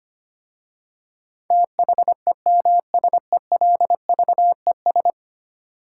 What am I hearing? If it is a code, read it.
Morse code, heard as a keyed tone, about 25 wpm, 700 Hz. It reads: THEMSELVES
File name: 10 Letter Words - First 3 Letters Given - 25wpm